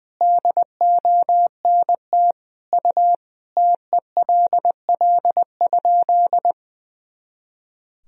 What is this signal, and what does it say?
Morse code, heard as a keyed tone, about 20 wpm, 700 Hz. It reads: DONT U TELL?
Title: Sentences from Top 300 Words and Abbreviations 20wpm